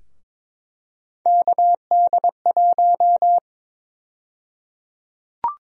Morse code, heard as a keyed tone, about 21 wpm, 700 Hz. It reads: KD1
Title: USA Call Sign Prefixes - Repeated 2x - 22wpm